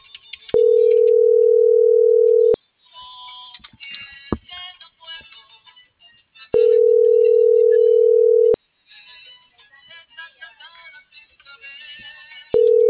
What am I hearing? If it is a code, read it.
Ringback tone (North America), 440 + 480 Hz.